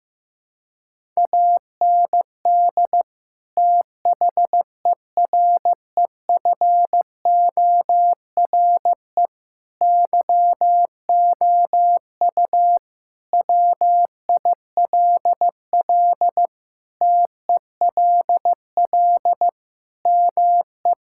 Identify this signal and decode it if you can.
Morse code, heard as a keyed tone, about 16 wpm, 700 Hz. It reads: AND THEREFORE YOU WILL TELL ME
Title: Sentences from Top 1000 Words - Review Part 5 - 15wpm